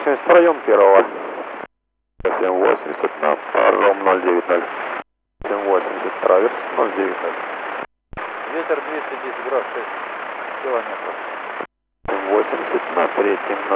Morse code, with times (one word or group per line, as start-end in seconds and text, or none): none